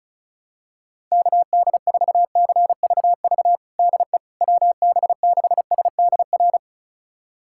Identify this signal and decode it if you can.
Morse code, heard as a keyed tone, about 36 wpm, 700 Hz. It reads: KD4CVV DE WB6SDR
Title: USA Callsign DE Callsign - 6 Characters - 35wpm